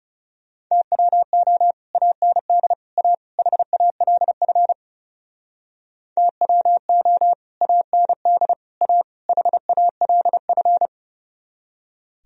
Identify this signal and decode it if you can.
Morse code, heard as a keyed tone, about 36 wpm, 700 Hz. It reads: TWO AND A HALF TWO AND A HALF
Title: Speed-Racing - Sets of 4 Words 30wpm